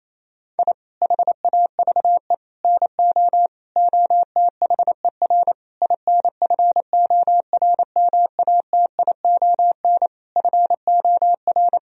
Morse code, heard as a keyed tone, about 28 wpm, 700 Hz. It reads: I HAVE NO OTHER INFORMATION FOR
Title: Sentences from Top 1000 Words - Review Part 3 - 28wpm